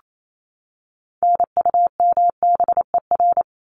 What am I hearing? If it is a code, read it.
Morse code, heard as a keyed tone, about 28 wpm, 700 Hz. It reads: NUMBER
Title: ES:S Suffix - Top 100 Words with Base Words 28wpm